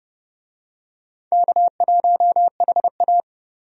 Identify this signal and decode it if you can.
Morse code, heard as a keyed tone, about 30 wpm, 700 Hz. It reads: K1HA
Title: USA Call Signs - 4 Characters - 30wpm